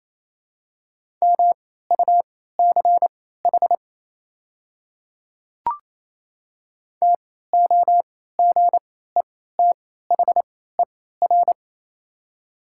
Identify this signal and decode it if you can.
Morse code, heard as a keyed tone, about 28 wpm, spaced as for 16 wpm, 700 Hz. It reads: MUCH TOGETHER
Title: 3x Character Spacing - Top 500 Words - 28wpm